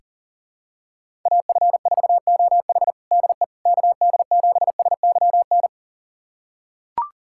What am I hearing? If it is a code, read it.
Morse code, heard as a keyed tone, about 40 wpm, 700 Hz. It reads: AF4OH DE KD7SYN
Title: USA Callsign DE Callsign 40wpm